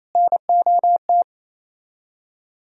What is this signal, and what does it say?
Morse code, heard as a keyed tone, about 34 wpm, 700 Hz. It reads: NOT
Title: Speed-Racing - Sets of 3 Words 28wpm